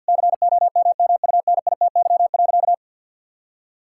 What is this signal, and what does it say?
Morse code, heard as a keyed tone, about 51 wpm, 700 Hz. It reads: COMMUNITY.
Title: Sentences from Top 1000 Words - Review Part 6 - 50wpm